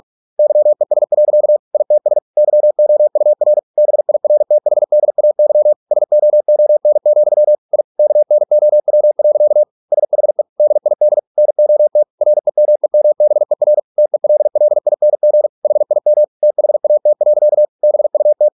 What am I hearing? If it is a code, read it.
Morse code, heard as a keyed tone, about 46 wpm, 600 Hz. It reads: YES. ITS YOUR BIRTHDAY SOON, I KNOW. SHE DID NOT REMEMBER TELLING HIM THAT. BUT